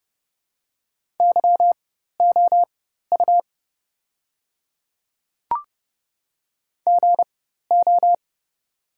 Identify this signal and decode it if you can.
Morse code, heard as a keyed tone, about 30 wpm, spaced as for 14 wpm, 700 Hz. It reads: YOU GO
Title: 4x Character Spacing - Top 200 Words - 30wpm